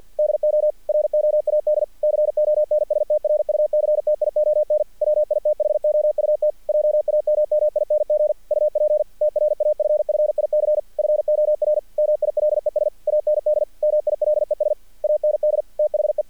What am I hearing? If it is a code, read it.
Morse code, heard as a keyed tone, about 48 wpm, 600 Hz. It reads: DO ROAD CONSTRUCTION WITHOUT JAMMING UP TRAFFIC FOR MILES AND MILES AND THE
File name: archived / CRSnet